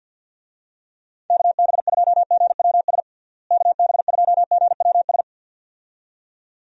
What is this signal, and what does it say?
Morse code, heard as a keyed tone, about 47 wpm, 700 Hz. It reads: KB2GWS KB2GWS
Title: USA Call Signs - 6 Characters - Repeated 2x - 50wpm